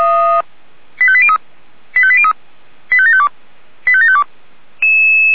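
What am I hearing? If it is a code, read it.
Five-tone selective call (ZVEI), heard as quick runs of five tones, tones of about 70 ms. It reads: 85803 85803 85752 85752
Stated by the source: ZVEI five-tone call received on SDR.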